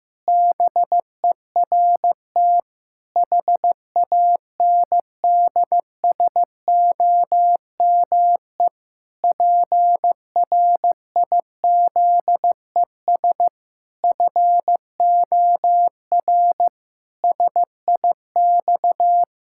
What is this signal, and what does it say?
Morse code, heard as a keyed tone, about 15 wpm, 700 Hz. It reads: BERT HANDSOME PRIZES FOR SIX
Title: Pangram Challenge v2 - 15wpm